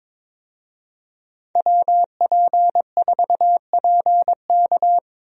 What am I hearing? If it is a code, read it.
Morse code, heard as a keyed tone, about 22 wpm, 700 Hz. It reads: WP4PK
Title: USA Call Signs - 5 Characters - Repeated 2x - 22wpm